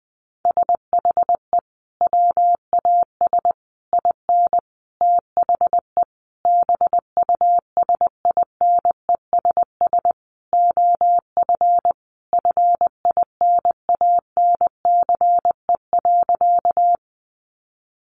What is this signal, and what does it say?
Morse code, heard as a keyed tone, about 19 wpm, 700 Hz. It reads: SHE WAS IN THE BUSINESS OF FINANCE.
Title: Sentences from Top 1500 Words 20wpm